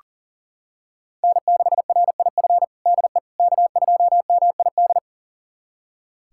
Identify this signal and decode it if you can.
Morse code, heard as a keyed tone, about 40 wpm, 700 Hz. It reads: N6RIF DE K2MID
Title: USA Callsign DE Callsign - 5 Characters - 40wpm